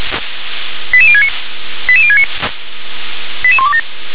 Five-tone selective call (ZVEI), heard as quick runs of five tones, tones of about 70 ms. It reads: 88069 88069 88127